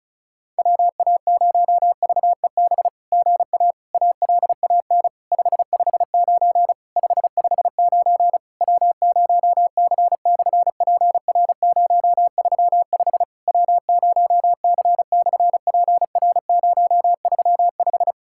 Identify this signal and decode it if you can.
Morse code, heard as a keyed tone, about 36 wpm, 700 Hz. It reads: WA0VEB GA ALAN 559 559 W0C/PR035 W0C/PR035
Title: SOTA QSO Practice 35wpm